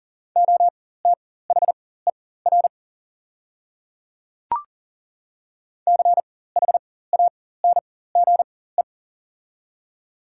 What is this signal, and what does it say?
Morse code, heard as a keyed tone, about 40 wpm, spaced as for 19 wpm, 700 Hz. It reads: OTHER CHANGE
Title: 4x Character Spacing - Top 300 Words - 40wpm